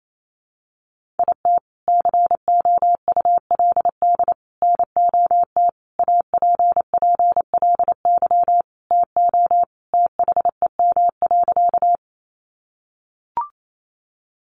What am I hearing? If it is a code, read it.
Morse code, heard as a keyed tone, about 28 wpm, 700 Hz. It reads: IT COULD NOT APPLY TO THEM.
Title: Sentences from Top 1200 Words 28wpm